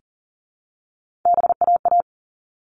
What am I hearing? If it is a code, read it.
Morse code, heard as a keyed tone, about 40 wpm, 700 Hz. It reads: BAA
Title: All Cut Numbers - 3-Digit 40wpm